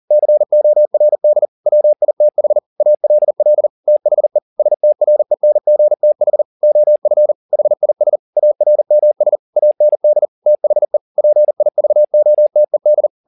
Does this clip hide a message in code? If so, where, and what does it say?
Yes: Morse code from 0.1 to 13.1 s, heard as a keyed tone, about 40 wpm, 600 Hz. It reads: CORD WITH ALL THE STRENGTH OF HIS ARMS AND THE PIVOTED